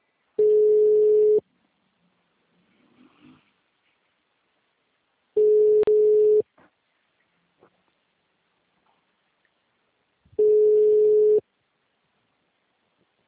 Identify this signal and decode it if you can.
Ringback tone (Europe, 425 Hz).
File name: out- -2011- - - .39